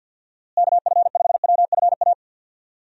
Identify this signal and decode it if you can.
Morse code, heard as a keyed tone, about 51 wpm, 700 Hz. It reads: KF5WFA
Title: USA Call Signs - 6 Characters - Repeated 2x - 50wpm